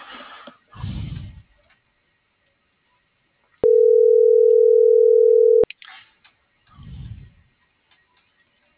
Ringback tone (North America), 440 + 480 Hz.